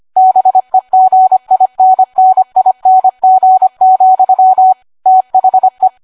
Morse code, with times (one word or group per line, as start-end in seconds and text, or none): 0.2-4.7 BEGINNING,
5.1-5.9 THE